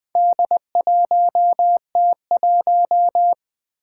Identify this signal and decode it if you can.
Morse code, heard as a keyed tone, about 20 wpm, 700 Hz. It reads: D1T1
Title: Four Challenging Letter-Number Combinations 20wpm